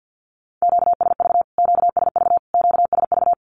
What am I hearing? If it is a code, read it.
Morse code, heard as a keyed tone, about 50 wpm, 700 Hz. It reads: QSV QSV QSV
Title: Q-Codes Repeated 3x 50wpm